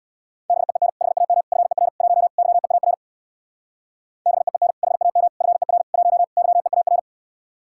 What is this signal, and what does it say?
Morse code, heard as a keyed tone, about 63 wpm, 700 Hz. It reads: BEEN HAM FER 1 YEAR BEEN HAM FER 1 YEAR